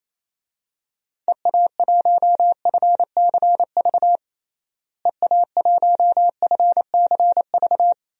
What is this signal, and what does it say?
Morse code, heard as a keyed tone, about 28 wpm, 700 Hz. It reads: EA1FCV EA1FCV
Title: DXCC-100 Call Signs - 6 Characters - Repeated 2x - 28wpm